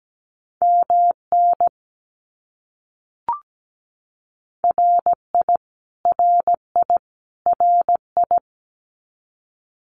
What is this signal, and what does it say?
Morse code, heard as a keyed tone, about 17 wpm, 700 Hz. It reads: MN RI RI RI
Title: US States Repeated 3x 17wpm